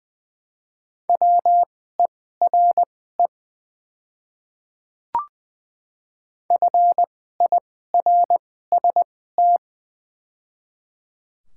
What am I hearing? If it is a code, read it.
Morse code, heard as a keyed tone, about 20 wpm, spaced as for 14 wpm, 700 Hz. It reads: WERE FIRST
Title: 2x Character Spacing - Top 100 Words - 20wpm